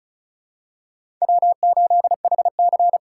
Morse code, heard as a keyed tone, about 36 wpm, 700 Hz. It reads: W8HC
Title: USA Call Signs - 2x Character Spacing - 35wpm